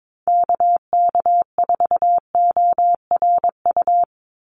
Morse code, heard as a keyed tone, about 22 wpm, 700 Hz. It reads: KK4ORU
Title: USA Call Signs 22wpm